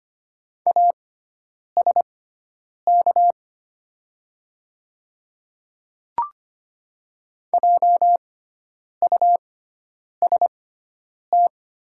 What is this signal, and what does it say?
Morse code, heard as a keyed tone, about 25 wpm, spaced as for 9 wpm, 700 Hz. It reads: ASK JUST
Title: 6x Character Spacing - Top 200 Words - 25wpm